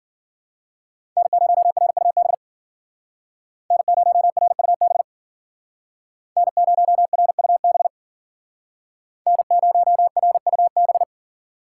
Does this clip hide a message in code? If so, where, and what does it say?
Yes: Morse code from 1.2 to 11.0 s, heard as a keyed tone, about 52 wpm, 700 Hz. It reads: N0RUB N0RUB N0RUB N0RUB